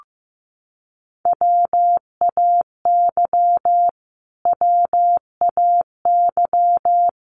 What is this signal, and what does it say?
Morse code, heard as a keyed tone, about 15 wpm, 700 Hz. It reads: WAY WAY
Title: Top 500 Words Repeated 15wpm